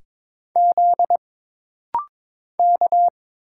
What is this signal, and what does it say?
Morse code, heard as a keyed tone, about 23 wpm, 700 Hz. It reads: Z K
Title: Single Letters - Rapid-Fire 22wpm